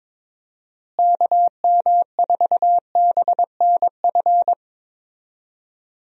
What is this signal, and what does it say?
Morse code, heard as a keyed tone, about 22 wpm, 700 Hz. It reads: KM4BNF
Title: USA Call Signs - 6 Characters - Repeated 3x - 22wpm